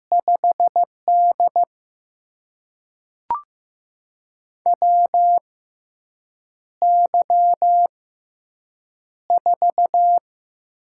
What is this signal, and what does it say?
Morse code, heard as a keyed tone, about 15 wpm, spaced as for 7 wpm, 700 Hz. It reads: 5D WY4
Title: USA Call Signs - 6x Character Spacing - 15wpm